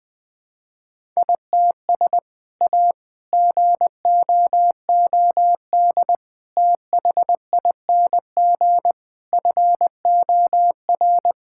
Morse code, heard as a keyed tone, about 20 wpm, 700 Hz. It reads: ITS A GOOD THING FOR
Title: Sentences from Top 1000 Words - Review Part 5 - 20wpm